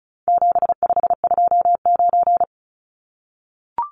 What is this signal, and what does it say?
Morse code, heard as a keyed tone, about 34 wpm, 700 Hz. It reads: 7529
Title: Numbers - 4-Digit 35wpm